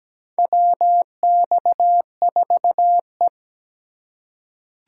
Morse code, heard as a keyed tone, about 17 wpm, 700 Hz. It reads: WX4E
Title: USA Call Signs - 4 Characters - 2x Character Spacing - 17wpm